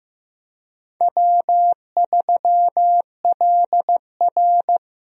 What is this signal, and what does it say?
Morse code, heard as a keyed tone, about 16 wpm, 700 Hz. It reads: W3LR
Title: USA Call Signs - 4 Characters - Repeated 3x - 15wpm